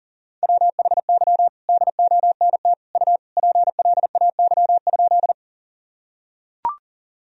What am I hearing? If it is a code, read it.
Morse code, heard as a keyed tone, about 40 wpm, 700 Hz. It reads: WHY DONT U PLAY?
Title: Sentences from Top 300 Words and Abbreviations 40wpm